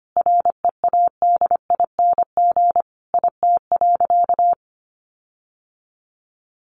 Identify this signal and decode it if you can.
Morse code, heard as a keyed tone, about 25 wpm, 700 Hz. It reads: READING IT.
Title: Sentences from Top 1300 Words 25wpm